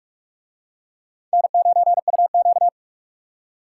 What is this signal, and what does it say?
Morse code, heard as a keyed tone, about 42 wpm, 700 Hz. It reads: N9UQ